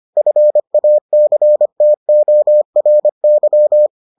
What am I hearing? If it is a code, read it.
Morse code, heard as a keyed tone, about 25 wpm, 600 Hz. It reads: FACTORY